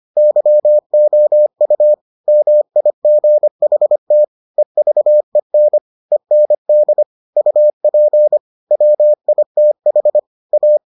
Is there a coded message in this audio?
Yes: Morse code from 0.2 to 10.8 s, heard as a keyed tone, about 25 wpm, 600 Hz. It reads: YOU MIGHT EVEN END UP WITH A